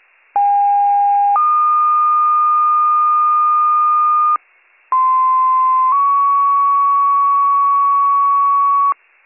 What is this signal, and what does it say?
Two-tone page, heard as a one-second tone, then a three-second tone, about 810 then 1220 Hz, about 1040 then 1120 Hz. It is heard from 0.4 to 8.9 s.